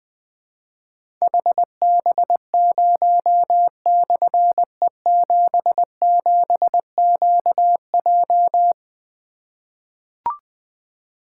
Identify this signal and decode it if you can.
Morse code, heard as a keyed tone, about 20 wpm, 700 Hz. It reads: HB0/E77QJ
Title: Difficult Call Signs 20wpm